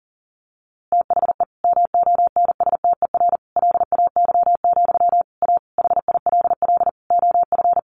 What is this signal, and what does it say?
Morse code, heard as a keyed tone, about 40 wpm, 700 Hz. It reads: THE MONSTER LAY, A HILL OF
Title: A Sound Of Thunder 40wpm